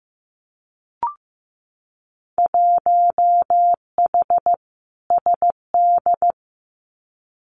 Morse code, heard as a keyed tone, about 15 wpm, 700 Hz. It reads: 1H SD
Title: Winter Field Day Exchanges 15wpm